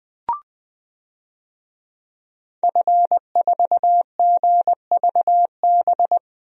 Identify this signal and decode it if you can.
Morse code, heard as a keyed tone, about 21 wpm, 700 Hz. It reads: F4GVB
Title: DXCC-100 Call Signs - 5 Characters - Repeated 3x - 20wpm